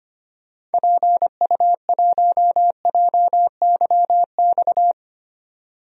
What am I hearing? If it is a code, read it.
Morse code, heard as a keyed tone, about 25 wpm, 700 Hz. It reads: PU1JYX
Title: DXCC-100 Call Signs - 6 Characters - 25wpm